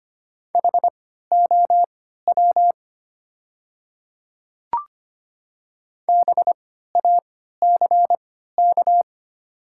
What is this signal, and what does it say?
Morse code, heard as a keyed tone, about 25 wpm, spaced as for 14 wpm, 700 Hz. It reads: HOW BACK
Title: 3x Character Spacing - Top 300 Words - 25wpm